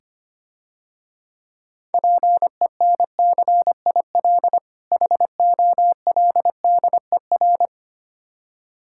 Morse code, heard as a keyed tone, about 25 wpm, 700 Hz. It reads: PENCIL HOLDER
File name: Easy Words - Speed-Racing with Hints - 25wpm